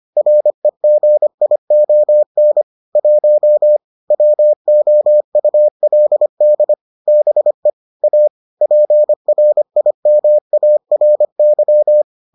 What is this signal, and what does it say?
Morse code, heard as a keyed tone, about 25 wpm, 600 Hz. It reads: REGION 1 WOULD BE A PRIMARY